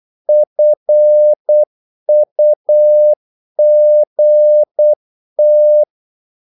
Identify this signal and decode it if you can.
Morse code, heard as a keyed tone, about 8 wpm, 600 Hz. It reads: FUGT
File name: Korte ord DK 8wpm